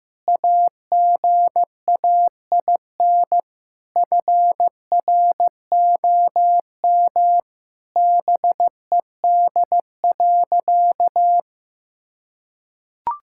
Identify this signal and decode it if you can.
Morse code, heard as a keyed tone, about 15 wpm, 700 Hz. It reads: AGAIN FROM BED.
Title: Sentences from Top 1000 Words - Review Part 6 - 15wpm